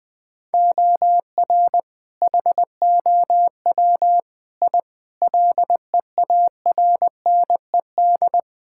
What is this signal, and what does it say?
Morse code, heard as a keyed tone, about 20 wpm, 700 Hz. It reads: OR HOW I LEARNED
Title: Sentences with 8-Word Phrases - Spoken Beginning - 20wpm